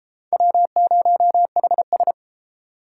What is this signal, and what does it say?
Morse code, heard as a keyed tone, about 28 wpm, 700 Hz. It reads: W0HS